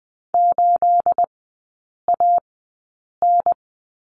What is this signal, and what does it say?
Morse code, heard as a keyed tone, about 20 wpm, spaced as for 8 wpm, 690 Hz. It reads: 8AN
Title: 212 Lesson 34e - Number 8 - Three Characters 20wpm